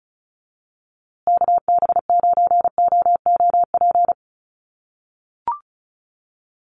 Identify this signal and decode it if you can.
Morse code, heard as a keyed tone, about 34 wpm, 700 Hz. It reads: KB9OOP
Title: USA Call Signs Repeated 3x 35wpm